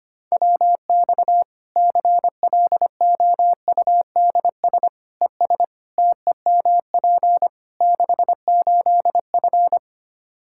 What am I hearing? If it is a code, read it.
Morse code, heard as a keyed tone, about 25 wpm, 700 Hz. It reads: WX CLOUDS ES TEMP 68F
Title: Practice QSO - Exchange 2 - Info - 25wpm